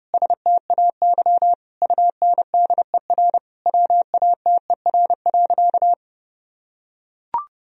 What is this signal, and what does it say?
Morse code, heard as a keyed tone, about 30 wpm, 700 Hz. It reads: STAY UNDER WATER.
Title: Sentences from Top 1000 Words - Review Part 2 - 30wpm